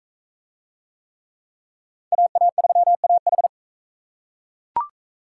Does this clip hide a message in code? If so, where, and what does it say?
Yes: Morse code from 2.1 to 3.5 s, heard as a keyed tone, about 36 wpm, 700 Hz. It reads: AA3AH